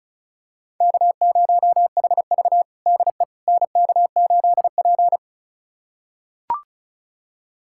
Morse code, heard as a keyed tone, about 36 wpm, 700 Hz. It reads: K0HV DE NK8P
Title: USA Callsign DE Callsign - 4 Characters - 35wpm